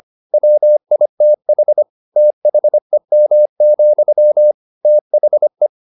Morse code, heard as a keyed tone, about 25 wpm, 600 Hz. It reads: WITH THEM, THE
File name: the old man and the sea part 1 25wpm